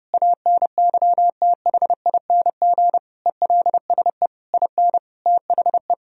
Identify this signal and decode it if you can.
Morse code, heard as a keyed tone, about 30 wpm, 700 Hz. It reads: ANYTHING ELSE IN THE
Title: Sentences from Top 1000 Words - Review Part 4 - 30wpm